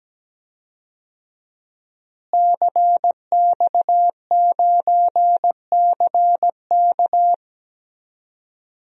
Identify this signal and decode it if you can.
Morse code, heard as a keyed tone, about 17 wpm, 700 Hz. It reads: CX9CK
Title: DXCC-100 Call Signs - 5 Characters - Repeated 2x - 17wpm